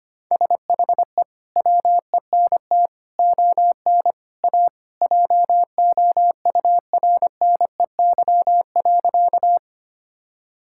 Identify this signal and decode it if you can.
Morse code, heard as a keyed tone, about 25 wpm, 700 Hz. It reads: SHE WENT ON A JOURNEY.
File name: Sentences with 2-Word Phrases - 25wpm